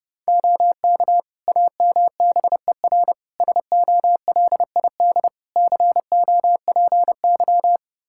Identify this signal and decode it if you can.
Morse code, heard as a keyed tone, about 30 wpm, 700 Hz. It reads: OK AMBER SOLID COPY
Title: Practice QSO - Exchange 3 - More Info - 30wpm